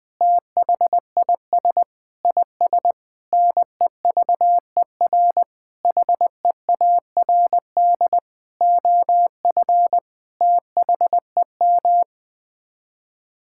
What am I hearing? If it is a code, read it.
Morse code, heard as a keyed tone, about 20 wpm, 700 Hz. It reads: THIS IS NEVER HEARD OF THEM
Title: Sets of 9 Words - 20wpm